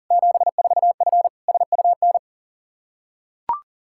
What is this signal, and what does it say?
Morse code, heard as a keyed tone, about 40 wpm, 700 Hz. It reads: 74F SUN
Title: Practice QSO - Exchange 2 - Info - 40wpm